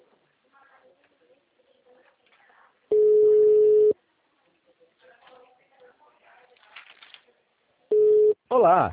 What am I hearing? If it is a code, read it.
Ringback tone (Europe, 425 Hz), with speech.